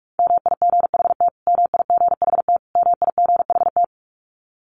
Morse code, heard as a keyed tone, about 44 wpm, 700 Hz. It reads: MIGHT MIGHT MIGHT
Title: Top 500 Words Repeated 45wpm